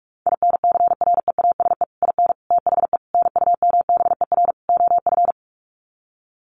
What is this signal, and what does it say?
Morse code, heard as a keyed tone, about 44 wpm, 700 Hz. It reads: INCREASE IN THE NUMBER OF
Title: Sets of 5 Words 45wpm